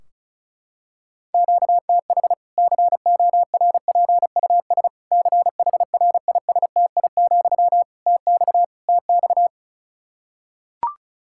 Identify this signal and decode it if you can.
Morse code, heard as a keyed tone, about 34 wpm, 700 Hz. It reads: QTH CORPUS CHRISTI, TX TX
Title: Practice QSO - Exchange 1 - Beginning - 35wpm